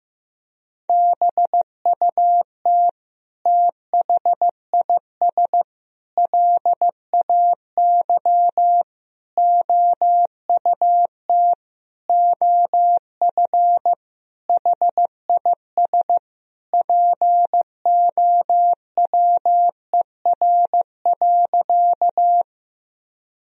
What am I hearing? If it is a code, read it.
Morse code, heard as a keyed tone, about 15 wpm, 700 Hz. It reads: BUT THIS LAY OUT OF HIS POWER.
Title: Sentences from Top 1000 Words - Review Part 2 - 15wpm